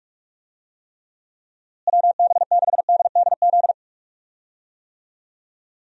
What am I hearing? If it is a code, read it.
Morse code, heard as a keyed tone, about 42 wpm, 700 Hz. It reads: WB6DDZ
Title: USA Call Signs - 6 Characters - Repeated 3x - 45wpm